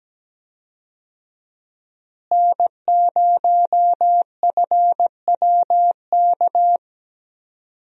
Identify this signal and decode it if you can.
Morse code, heard as a keyed tone, about 17 wpm, 700 Hz. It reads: N0FWK
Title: USA Call Signs - 5 Characters - Repeated 2x - 17wpm